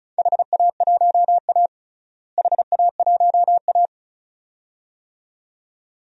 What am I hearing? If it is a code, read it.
Morse code, heard as a keyed tone, about 36 wpm, 700 Hz. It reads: HA1A HA1A
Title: DXCC-100 Call Signs - 4 Characters - Repeated 2x - 35wpm